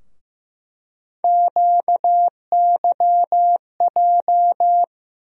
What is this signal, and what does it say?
Morse code, heard as a keyed tone, about 15 wpm, 700 Hz. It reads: QYJ
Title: USA Call Sign Postfixes - 3 Letters 15wpm